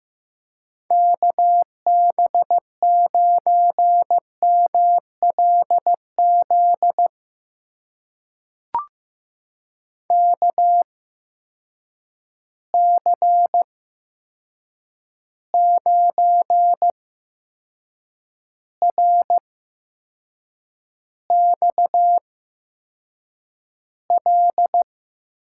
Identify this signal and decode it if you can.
Morse code, heard as a keyed tone, about 15 wpm, spaced as for 6 wpm, 700 Hz. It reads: KB9MLZ KC9RXL
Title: USA Call Signs - 8x Character Spacing - 15wpm